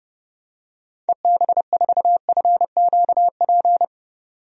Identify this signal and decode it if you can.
Morse code, heard as a keyed tone, about 29 wpm, 700 Hz. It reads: EB4FQP
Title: DXCC-100 Call Signs - 6 Characters - 30wpm